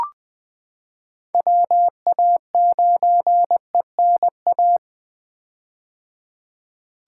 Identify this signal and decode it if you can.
Morse code, heard as a keyed tone, about 20 wpm, 700 Hz. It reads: WA9ENA
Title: K1USN SST QSO Practice - 20wpm